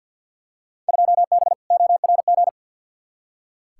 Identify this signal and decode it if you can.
Morse code, heard as a keyed tone, about 46 wpm, 700 Hz. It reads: 2B ORG
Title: Field Day Exchanges Encore - 50wpm